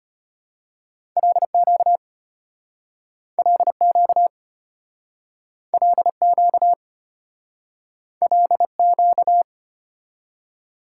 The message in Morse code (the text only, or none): LQ LQ LQ LQ